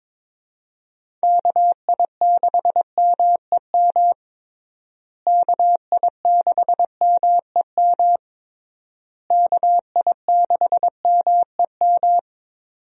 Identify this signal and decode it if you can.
Morse code, heard as a keyed tone, about 22 wpm, 700 Hz. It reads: KI6MEM KI6MEM KI6MEM
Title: USA Call Signs - 6 Characters - Repeated 3x - 22wpm